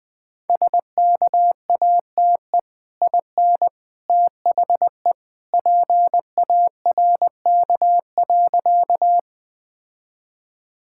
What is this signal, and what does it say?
Morse code, heard as a keyed tone, about 20 wpm, 700 Hz. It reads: SKATE IN THE PARK.
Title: Easy Sentences with Hints - 20wpm